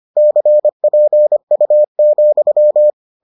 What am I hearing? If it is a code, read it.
Morse code, heard as a keyed tone, about 25 wpm, 600 Hz. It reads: CPU,